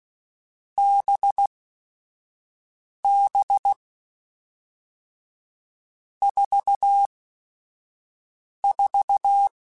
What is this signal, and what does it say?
Morse code, heard as a keyed tone, about 16 wpm, 800 Hz. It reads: B B 4 4
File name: Groups@5 / Groups@5 mp3s / VB46@5